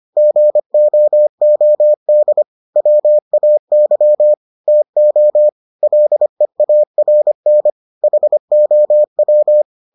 Morse code, heard as a keyed tone, about 25 wpm, 600 Hz. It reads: GOOD WAY TO LEARN HOW